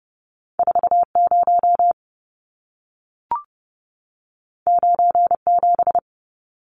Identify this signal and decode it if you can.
Morse code, heard as a keyed tone, about 30 wpm, 700 Hz. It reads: 40 97
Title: Numbers - 2-Digit 30wpm